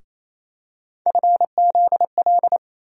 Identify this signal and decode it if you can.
Morse code, heard as a keyed tone, about 28 wpm, 700 Hz. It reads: FZL
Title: USA Call Sign Postfixes - 3 Letters 28wpm